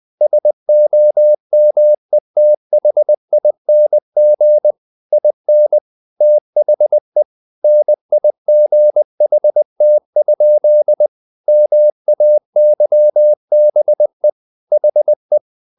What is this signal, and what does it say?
Morse code, heard as a keyed tone, about 20 wpm, 600 Hz. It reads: SOMETHING IN THE NIGHT? MAYBE HE
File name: the old man and the sea part 2 20wpm